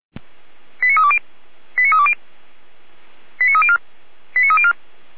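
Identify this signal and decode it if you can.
Five-tone selective call (ZVEI), heard as quick runs of five tones, tones of about 80 ms. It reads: 89320 89320 89304 89304
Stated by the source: ZVEI five-tone selective call received on an SDR.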